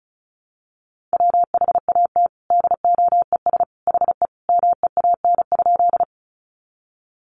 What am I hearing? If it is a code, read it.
Morse code, heard as a keyed tone, about 34 wpm, 700 Hz. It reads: WHAT DOES HE MEAN?
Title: Sentences from Top 200 Words 35wpm